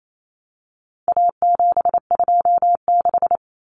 Morse code, heard as a keyed tone, about 28 wpm, 700 Hz. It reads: A726
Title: Common Cut Numbers - 4-Digit 28wpm